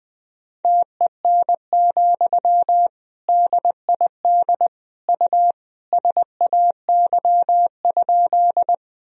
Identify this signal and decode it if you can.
Morse code, heard as a keyed tone, about 21 wpm, 700 Hz. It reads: TEN, DID U SAY?
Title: Sentences from Top 400 Words and Abbreviations 20wpm